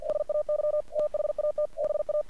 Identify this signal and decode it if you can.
Morse code, heard as a keyed tone, about 47 wpm, 600 Hz. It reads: SAY THAT BA
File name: archived / CRSnet